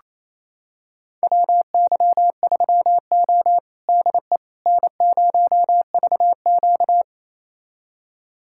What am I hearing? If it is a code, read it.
Morse code, heard as a keyed tone, about 28 wpm, 700 Hz. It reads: WY3O DE N0VQ
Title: USA Callsign DE Callsign - 4 Characters - 28wpm